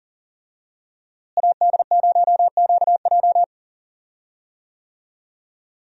Morse code, heard as a keyed tone, about 40 wpm, 700 Hz. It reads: AD0QJ
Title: USA Call Signs - 5 Characters - Repeated 2x - 40wpm